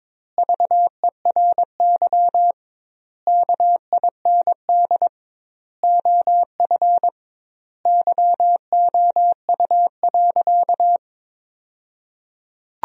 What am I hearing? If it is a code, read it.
Morse code, heard as a keyed tone, about 22 wpm, 700 Hz. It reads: VERY KIND OF YOU.
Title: 2x Word-Spacing - Sentences from Top 200 Words - 22wpm